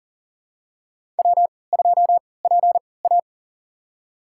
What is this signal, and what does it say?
Morse code, heard as a keyed tone, about 40 wpm, spaced as for 23 wpm, 700 Hz. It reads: W2PA
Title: USA Call Signs - 4 Characters - 3x Character Spacing - 40wpm